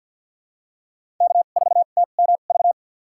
Morse code, heard as a keyed tone, about 52 wpm, spaced as for 37 wpm, 690 Hz. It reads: K4TMV
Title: USA Call Signs - 5 Characters - 2x Character Spacing - 50wpm